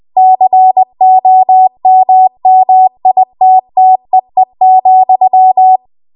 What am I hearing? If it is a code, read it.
Morse code, heard as a keyed tone, about 20 wpm, 750 Hz. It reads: COMMITTEE,
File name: Archive / 20 WPM / 20WPM